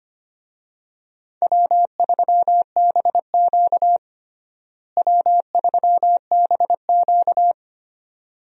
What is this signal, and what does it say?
Morse code, heard as a keyed tone, about 25 wpm, 700 Hz. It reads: W3BQ W3BQ
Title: USA Call Signs - 4 Characters - Repeated 2x - 25wpm